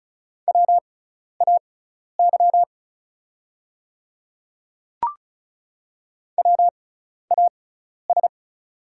Morse code, heard as a keyed tone, about 34 wpm, spaced as for 12 wpm, 700 Hz. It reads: WAY WAS
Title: 6x Character Spacing - Top 100 Words - 35wpm